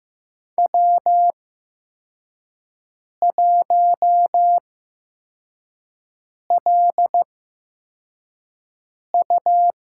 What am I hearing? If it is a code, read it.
Morse code, heard as a keyed tone, about 15 wpm, spaced as for 4 wpm, 700 Hz. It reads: W1LU
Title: USA Call Signs - 5 Characters - 8x Character Spacing - 15wpm